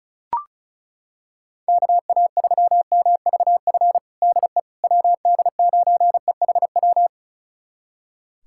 Morse code, heard as a keyed tone, about 34 wpm, 700 Hz. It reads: KA3MVF DE WD9EHW
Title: USA Callsign DE Callsign 35wpm